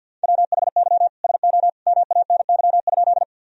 Morse code, heard as a keyed tone, about 53 wpm, 700 Hz. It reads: WHY SO MANY?
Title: Sentences from Top 1000 Words - Review Part 1 - 50wpm